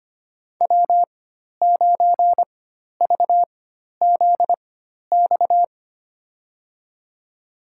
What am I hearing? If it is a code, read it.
Morse code, heard as a keyed tone, about 25 wpm, spaced as for 12 wpm, 700 Hz. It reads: W9VZX
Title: USA Call Signs - 4x Character Spacing - 25wpm